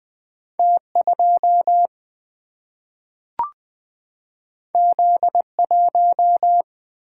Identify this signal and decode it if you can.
Morse code, heard as a keyed tone, about 20 wpm, 700 Hz. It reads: T2 Z1
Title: Single-Letter Single-Number 20wpm